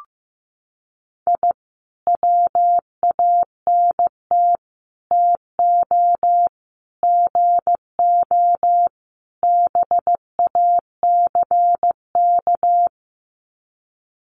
Morse code, heard as a keyed tone, about 15 wpm, 700 Hz. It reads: I WANT TO GO BACK
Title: Sets of 5 Words Encore 15wpm